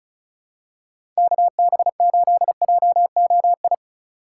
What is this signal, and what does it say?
Morse code, heard as a keyed tone, about 36 wpm, 700 Hz. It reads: KB8JOI
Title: USA Call Signs - 6 Characters - 4x Character Spacing - 35wpm